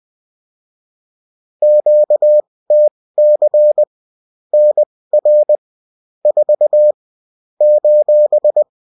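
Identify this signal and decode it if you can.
Morse code, heard as a keyed tone, about 20 wpm, spaced as for 16 wpm, 600 Hz. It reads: QTC NR 4 :
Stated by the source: Morse code bulletin of the UFT.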